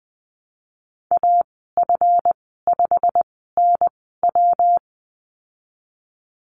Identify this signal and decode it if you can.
Morse code, heard as a keyed tone, about 20 wpm, spaced as for 15 wpm, 700 Hz. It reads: AF5NW
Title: 167 Lesson 27g - Number 7 - Call Signs 20wpm